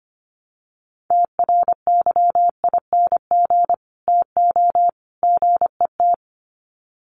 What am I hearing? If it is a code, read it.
Morse code, heard as a keyed tone, about 25 wpm, 700 Hz. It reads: TRYING TO GET
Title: Sets of 3 Words 25wpm